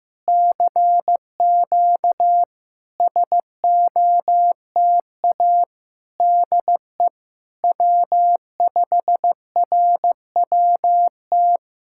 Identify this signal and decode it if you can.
Morse code, heard as a keyed tone, about 15 wpm, 700 Hz. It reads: CQ SOTA DE W5RWT
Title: SOTA QSO Practice 15wpm